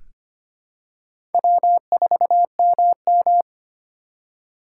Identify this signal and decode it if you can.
Morse code, heard as a keyed tone, about 25 wpm, 700 Hz. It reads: W4MM
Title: USA Call Signs - 4 Characters - 4x Character Spacing - 25wpm